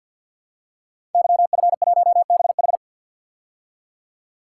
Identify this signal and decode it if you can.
Morse code, heard as a keyed tone, about 52 wpm, 700 Hz. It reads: YF1BH
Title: Code Groups - 5 Letter:Numbers - 50wpm